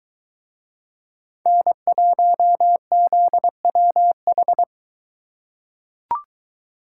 Morse code, heard as a keyed tone, about 22 wpm, 700 Hz. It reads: N1ZWH